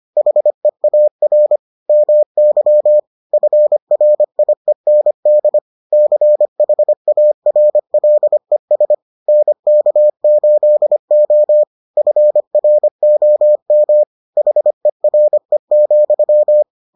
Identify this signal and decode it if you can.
Morse code, heard as a keyed tone, about 25 wpm, 600 Hz. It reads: HEAR MY FRIEND CHARLES NK8O FROM HERE,